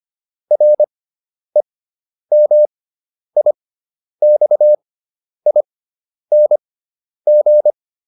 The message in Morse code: REMIXING

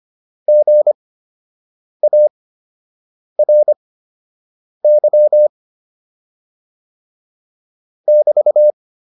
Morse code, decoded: GARY <BT>

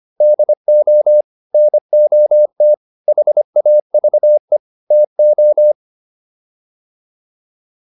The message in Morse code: DO NOT HAVE TO

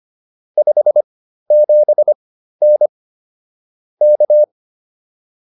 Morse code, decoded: 57N K

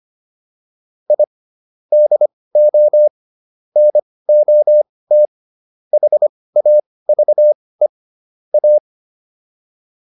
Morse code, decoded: I DO NOT HAVE A

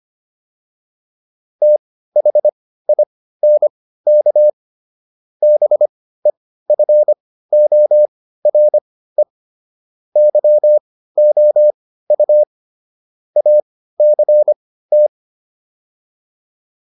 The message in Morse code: THINK BEFORE YOU ACT

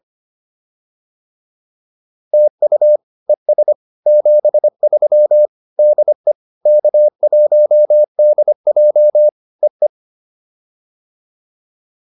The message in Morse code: TU ES 73 DE K1DJ E E

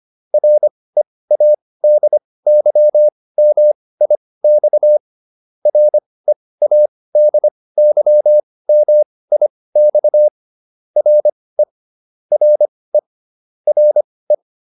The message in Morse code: READYMIX READYMIX RE RE RE

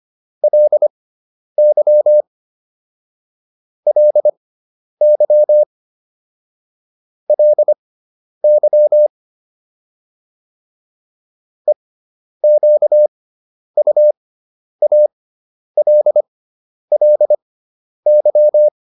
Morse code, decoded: LY LY LY EQUALLY

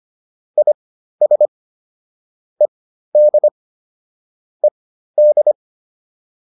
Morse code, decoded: IS ED ED

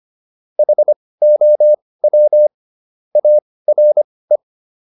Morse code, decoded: HOW ARE

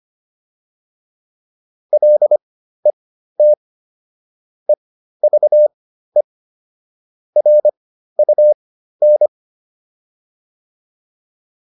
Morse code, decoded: LET EVE RUN

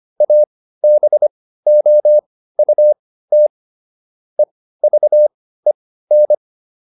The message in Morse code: ABOUT EVEN

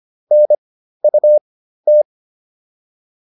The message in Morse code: NUT

